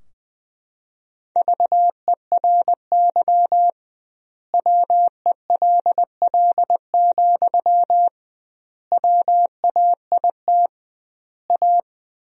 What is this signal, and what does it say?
Morse code, heard as a keyed tone, about 20 wpm, 700 Hz. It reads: VERY WELL, WAIT A
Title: 2x Word-Spacing - Sentences from Top 500 Words - 20wpm